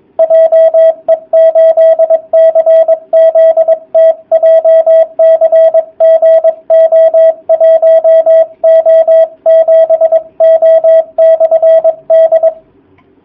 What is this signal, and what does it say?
Morse code, heard as a keyed tone, about 22 wpm, 660 Hz, with radio noise. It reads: JE8CZT JCG O1O7O/D
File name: jh8cbh / 2019cw / sp23- w